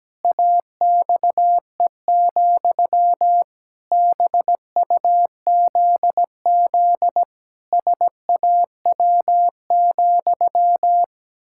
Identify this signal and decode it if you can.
Morse code, heard as a keyed tone, about 17 wpm, 700 Hz. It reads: AXE, BUZZ SAW,
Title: Pangram Challenge v2 - 17wpm